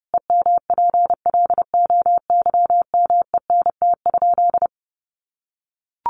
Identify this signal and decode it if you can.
Morse code, heard as a keyed tone, about 30 wpm, 700 Hz. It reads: EMPLOYMENT?
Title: Sentences from Top 1500 Words 30wpm